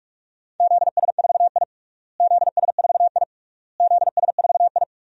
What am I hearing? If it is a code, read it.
Morse code, heard as a keyed tone, about 46 wpm, 700 Hz. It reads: ZS4I ZS4I ZS4I
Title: DXCC-100 Call Signs - 4 Characters - Repeated 3x - 45wpm